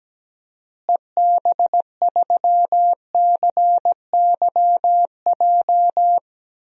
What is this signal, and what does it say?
Morse code, heard as a keyed tone, about 17 wpm, 700 Hz. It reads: EB3CYJ
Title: DXCC-100 Call Signs - 6 Characters - Repeated 2x - 17wpm